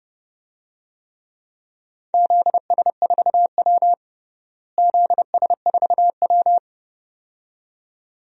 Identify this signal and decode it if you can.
Morse code, heard as a keyed tone, about 31 wpm, 700 Hz. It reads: ZS4W ZS4W
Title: DXCC-100 Call Signs - 4 Characters - Repeated 2x - 30wpm